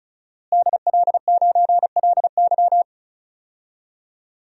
Morse code, heard as a keyed tone, about 36 wpm, 700 Hz. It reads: DL9LY
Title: DXCC-100 Call Signs - 5 Characters - Repeated 2x - 35wpm